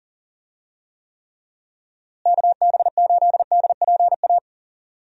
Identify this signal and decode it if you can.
Morse code, heard as a keyed tone, about 38 wpm, 700 Hz. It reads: KB8DPA
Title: USA Call Signs - 6 Characters - Repeated 2x - 40wpm